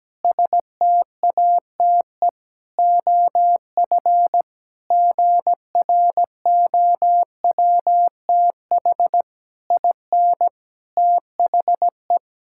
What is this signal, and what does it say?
Morse code, heard as a keyed tone, about 17 wpm, 700 Hz. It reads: STATE OF GROWTH IN THE
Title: Sentences from Top 1000 Words - Review Part 5 - 17wpm